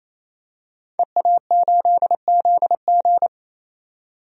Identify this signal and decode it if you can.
Morse code, heard as a keyed tone, about 27 wpm, 700 Hz. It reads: EA8ZG
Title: DXCC-100 Call Signs - 5 Characters - 28wpm